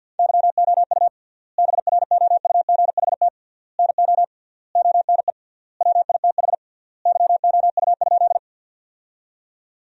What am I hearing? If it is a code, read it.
Morse code, heard as a keyed tone, about 52 wpm, 700 Hz. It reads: YOU BROUGHT NO ONE WITH YOU?
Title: 3x Word-Spacing - Sentences from Top 500 Words - 50wpm